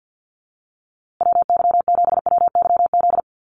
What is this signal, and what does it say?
Morse code, heard as a keyed tone, about 50 wpm, 700 Hz. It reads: WY7WYZ
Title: USA Call Signs Repeated 2x 50wpm